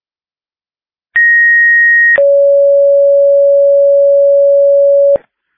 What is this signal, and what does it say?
Two-tone page, heard as a one-second tone, then a three-second tone, about 1900 then 560 Hz. It is heard from 1.2 to 5.2 s.